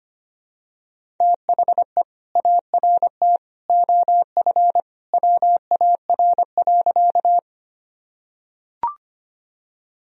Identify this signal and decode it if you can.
Morse code, heard as a keyed tone, about 25 wpm, 700 Hz. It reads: THE ART OF WAR.
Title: Sentences from Top 1000 Words - Review Part 3 - 25wpm